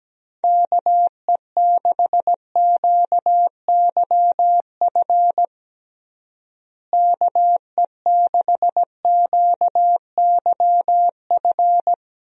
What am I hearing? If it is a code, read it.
Morse code, heard as a keyed tone, about 17 wpm, 700 Hz. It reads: KE6QYF KE6QYF
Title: USA Call Signs - 6 Characters - Repeated 3x - 17wpm